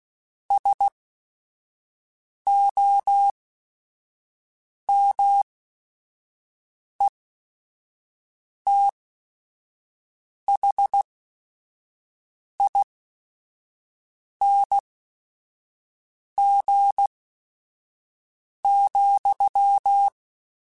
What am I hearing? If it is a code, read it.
Morse code, heard as a keyed tone, about 16 wpm, spaced as for 5 wpm, 800 Hz. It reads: SOMETHING,
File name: Quotes / Quotes-mp3s / StandFor @5